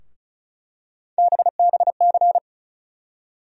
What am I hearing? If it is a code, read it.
Morse code, heard as a keyed tone, about 34 wpm, 700 Hz. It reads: BBC
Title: USA Call Sign Postfixes - 3 Letters 35wpm